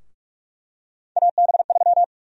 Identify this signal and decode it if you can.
Morse code, heard as a keyed tone, about 47 wpm, 700 Hz. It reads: AB3
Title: USA Call Sign Prefixes - 45wpm